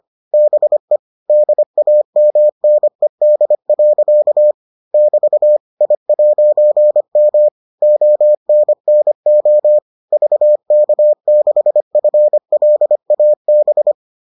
Morse code, heard as a keyed tone, about 25 wpm, 600 Hz. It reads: BE DAMNED. <BT> I'M ONNO VK6FLAB